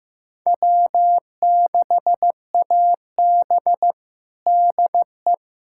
Morse code, heard as a keyed tone, about 15 wpm, 700 Hz. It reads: W6AB DE
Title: USA Callsign DE Callsign - 4 Characters - 15wpm